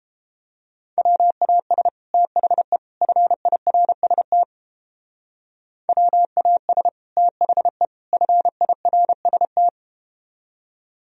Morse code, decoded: WAS THE FIRST WAS THE FIRST

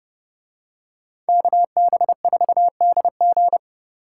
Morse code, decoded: KB4DG

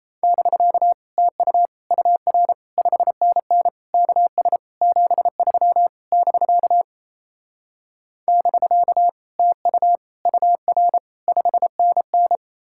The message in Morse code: <BK> TU UR 5NN KS 73 <BK> <BK> TU UR 5NN

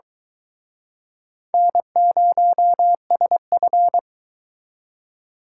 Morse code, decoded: N0SF

